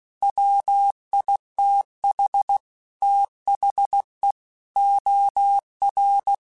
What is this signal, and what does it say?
Morse code, heard as a keyed tone, about 16 wpm, 800 Hz. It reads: WITH THE OR
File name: Technical Subjects / Technical Subjects-mp3s / OR Gate @15